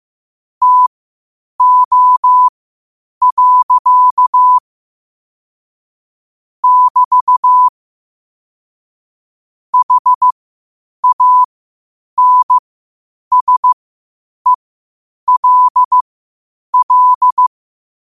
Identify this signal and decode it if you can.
Morse code, heard as a keyed tone, about 15 wpm, spaced as for 8 wpm, 1000 Hz. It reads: TO. <BT> HANSELL
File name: qotdfiles / 8wpm